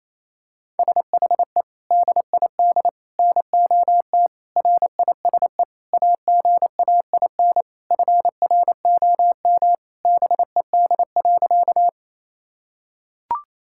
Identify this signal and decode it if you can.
Morse code, heard as a keyed tone, about 28 wpm, 700 Hz. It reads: SHE DID NOT RISE AGAIN FROM BED.
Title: Sentences from Top 1000 Words - Review Part 6 - 28wpm